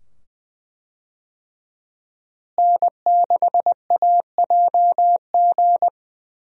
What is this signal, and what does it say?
Morse code, heard as a keyed tone, about 20 wpm, 700 Hz. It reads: N6AJG
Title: USA Call Signs - 5 Characters - Repeated 2x - 20wpm